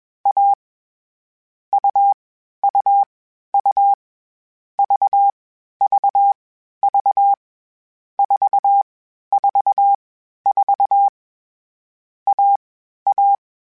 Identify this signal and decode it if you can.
Morse code, heard as a keyed tone, about 21 wpm, 800 Hz. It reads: A U U U V V V 4 4 4 A A
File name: Short Runs / Short Runs-mp3s / AUV AUV4 @20